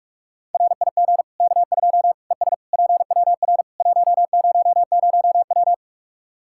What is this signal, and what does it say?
Morse code, heard as a keyed tone, about 41 wpm, 700 Hz. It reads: RIG K2 ES PWR 100W
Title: Practice QSO - Exchange 2 - Info - 45wpm